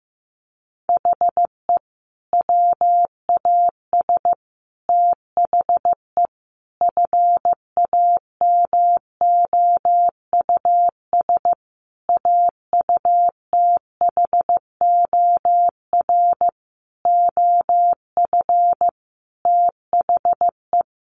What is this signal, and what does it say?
Morse code, heard as a keyed tone, about 15 wpm, 700 Hz. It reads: HE WAS THE FAMOUS AUTHOR OF THE
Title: Sentences from Top 1500 Words 15wpm